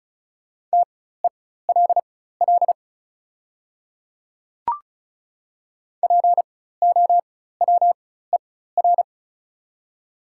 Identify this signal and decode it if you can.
Morse code, heard as a keyed tone, about 36 wpm, spaced as for 16 wpm, 690 Hz. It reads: TELL POWER
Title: 4x Character Spacing - Top 500 Words - 35wpm